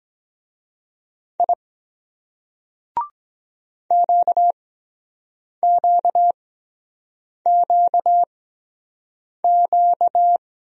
Morse code, heard as a keyed tone, about 24 wpm, spaced as for 7 wpm, 700 Hz. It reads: I QQQQ